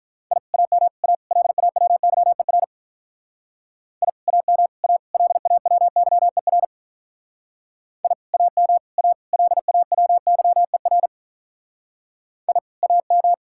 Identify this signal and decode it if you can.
Morse code, heard as a keyed tone, about 42 wpm, 700 Hz. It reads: I AM A LAWYER I AM A LAWYER I AM A LAWYER I AM
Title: Speed-Racing Common QSO Phrases Encore - 35wpm